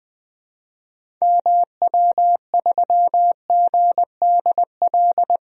Morse code, heard as a keyed tone, about 21 wpm, 700 Hz. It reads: MW3GDL
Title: DXCC-100 Call Signs - 6 Characters - 20wpm